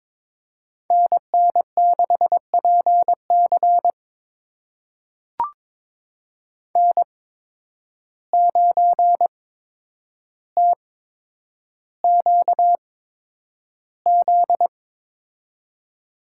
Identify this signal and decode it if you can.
Morse code, heard as a keyed tone, about 22 wpm, spaced as for 9 wpm, 700 Hz. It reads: NN6PC N9TQZ
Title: USA Call Signs - 5 Characters - 8x Character Spacing - 22wpm